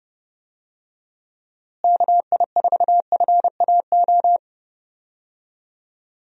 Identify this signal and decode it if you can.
Morse code, heard as a keyed tone, about 30 wpm, 700 Hz. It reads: KI4FAO
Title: USA Call Signs - 6 Characters - Repeated 3x - 30wpm